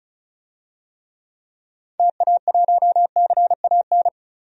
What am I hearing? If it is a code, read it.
Morse code, heard as a keyed tone, about 36 wpm, 700 Hz. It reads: TA1CAN
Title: DXCC-100 Call Signs - 6 Characters - 35wpm